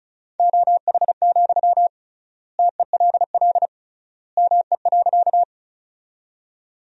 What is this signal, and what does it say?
Morse code, heard as a keyed tone, about 36 wpm, 700 Hz. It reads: OH, TELL ME.
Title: 3x Word-Spacing - Sentences from Top 500 Words - 35wpm